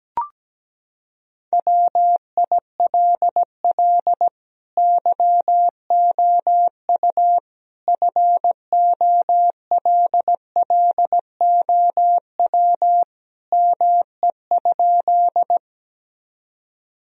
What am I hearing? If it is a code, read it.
Morse code, heard as a keyed tone, about 17 wpm, 700 Hz. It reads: WILL YOU FOLLOW ME?
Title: Sentences from Top 1000 Words - Review Part 1 - 17wpm